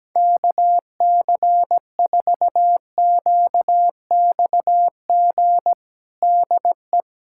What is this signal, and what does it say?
Morse code, heard as a keyed tone, about 17 wpm, 700 Hz. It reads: KC4QXG DE
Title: USA Callsign DE Callsign - 6 Characters - 17wpm